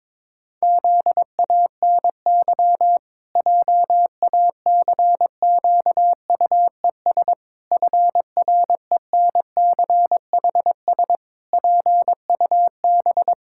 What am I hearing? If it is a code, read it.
Morse code, heard as a keyed tone, about 22 wpm, 700 Hz. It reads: ZANY JACQUES FRENCHS PUB
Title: Pangram Challenge v2 - 22wpm